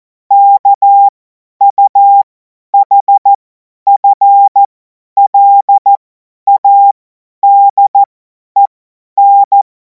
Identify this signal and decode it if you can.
Morse code, heard as a keyed tone, about 14 wpm, spaced as for 10 wpm, 800 Hz. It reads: KUHFLADEN
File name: Morsekurs / Witze / Witz 30. Juni 2026 50 F70